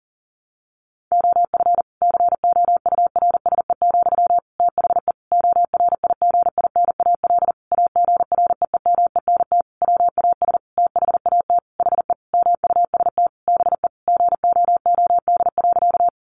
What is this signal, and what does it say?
Morse code, heard as a keyed tone, about 40 wpm, 700 Hz. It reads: OF COURSE, THE ORIGINAL AGREEMENT WAS THAT HE MUST BE GOOD.
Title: Sentences from Top 1500 Words 40wpm